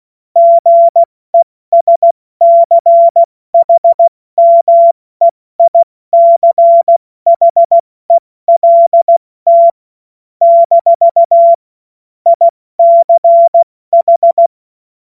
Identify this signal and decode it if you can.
Morse code, heard as a keyed tone, about 16 wpm, 680 Hz. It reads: GESCHMEICHELT - ICH
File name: Morsekurs / Witze / Witz 28. Dezember 2021, 70 F80